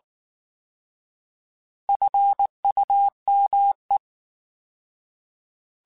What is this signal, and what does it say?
Morse code, heard as a keyed tone, about 19 wpm, 800 Hz. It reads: FUME
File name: Alphabetical / Alphabetical mp3s / Fs @18